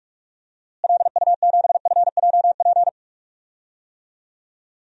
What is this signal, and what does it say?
Morse code, heard as a keyed tone, about 46 wpm, 700 Hz. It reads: LU7FJP
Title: DXCC-100 Call Signs - 6 Characters - Repeated 3x - 45wpm